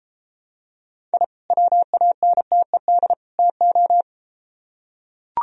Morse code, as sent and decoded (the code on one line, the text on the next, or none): .. / .-- .- -. - . -.. / - ---
I WANTED TO